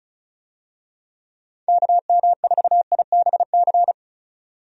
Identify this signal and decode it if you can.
Morse code, heard as a keyed tone, about 34 wpm, 700 Hz. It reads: KM4IBC
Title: USA Call Signs - 6 Characters - 35wpm